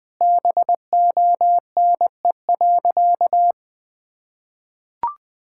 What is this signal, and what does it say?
Morse code, heard as a keyed tone, about 20 wpm, 700 Hz. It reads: BONE.
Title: Sentences from Top 1600 Words 20wpm